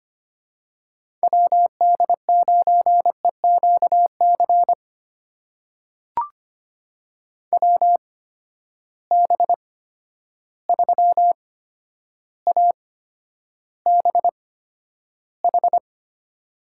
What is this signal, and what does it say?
Morse code, heard as a keyed tone, about 25 wpm, spaced as for 11 wpm, 700 Hz. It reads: WD9EQC WB3ABH
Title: USA Call Signs - 6 Characters - 8x Character Spacing - 25wpm